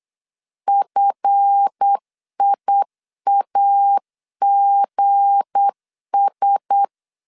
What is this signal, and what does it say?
Morse code, heard as a keyed tone, about 8 wpm, 800 Hz. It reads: FIAGS
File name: LEZIONI / 8 YJ45 / 8 Y 8wpm 3245